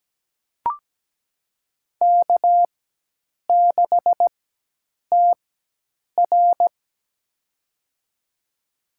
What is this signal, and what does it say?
Morse code, heard as a keyed tone, about 17 wpm, spaced as for 8 wpm, 700 Hz. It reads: K6TR
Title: USA Call Signs - 4 Characters - 4x Character Spacing - 17wpm